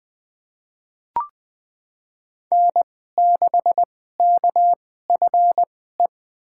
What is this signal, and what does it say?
Morse code, heard as a keyed tone, about 20 wpm, spaced as for 14 wpm, 700 Hz. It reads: N6KFE
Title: USA Call Signs - 5 Characters - 2x Character Spacing - 20wpm